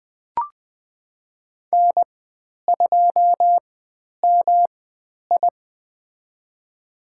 Morse code, heard as a keyed tone, about 20 wpm, spaced as for 10 wpm, 700 Hz. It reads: N2MI
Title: 109 Lesson 18g - Letter m - Call Signs 20wpm